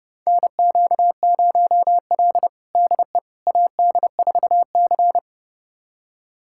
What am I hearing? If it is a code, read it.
Morse code, heard as a keyed tone, about 30 wpm, 700 Hz. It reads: NQ0L DE AD4C
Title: USA Callsign DE Callsign - 4 Characters - 30wpm